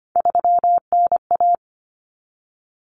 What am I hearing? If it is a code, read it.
Morse code, heard as a keyed tone, about 25 wpm, 700 Hz. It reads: 3NA
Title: Common Cut Numbers - 3-Digit 25wpm